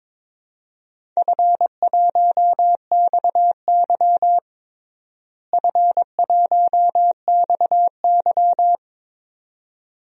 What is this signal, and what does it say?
Morse code, heard as a keyed tone, about 22 wpm, 700 Hz. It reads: F1XY F1XY
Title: DXCC-100 Call Signs - 4 Characters - Repeated 2x - 22wpm